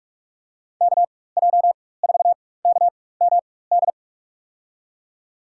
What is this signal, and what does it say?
Morse code, heard as a keyed tone, about 43 wpm, spaced as for 21 wpm, 700 Hz. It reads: KJ4KMD
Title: USA Call Signs - 4x Character Spacing - 45wpm